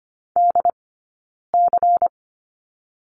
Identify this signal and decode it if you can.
Morse code, heard as a keyed tone, about 25 wpm, 700 Hz. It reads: D C
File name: course / 25:10wpm / 099 Lesson 17d - Letter u - Two Characters 25wpm